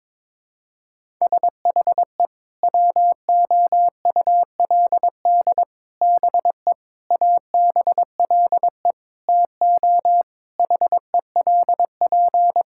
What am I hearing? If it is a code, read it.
Morse code, heard as a keyed tone, about 22 wpm, 700 Hz. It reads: SHE WOULD BE ABLE TO HELP
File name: practice / 22wpm / Sentences with 4-Word Phrases - 22wpm